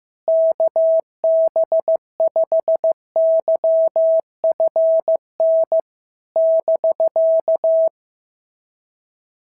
Morse code, heard as a keyed tone, about 14 wpm, 650 Hz. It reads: KB5YFN <BK>
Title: SOTA QSO Practice 15wpm